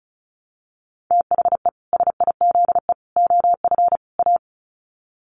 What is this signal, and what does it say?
Morse code, heard as a keyed tone, about 34 wpm, 700 Hz. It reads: THE SIZE OF A
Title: Sets of 4 Words 35wpm